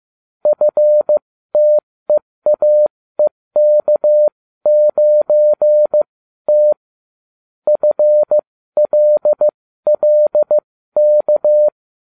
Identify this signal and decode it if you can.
Morse code, heard as a keyed tone, about 15 wpm, spaced as for 13 wpm, 600 Hz. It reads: FTEAEK9T FLLK